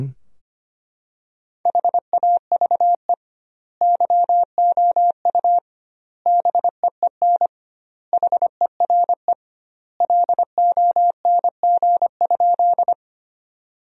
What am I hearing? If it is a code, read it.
Morse code, heard as a keyed tone, about 25 wpm, 700 Hz. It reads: HAVE YOU BEEN HERE LONG?
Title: 2x Word-Spacing - Sentences from Top 200 Words - 25wpm